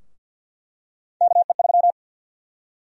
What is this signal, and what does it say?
Morse code, heard as a keyed tone, about 50 wpm, 700 Hz. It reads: KE3
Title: USA Call Sign Prefixes - 50wpm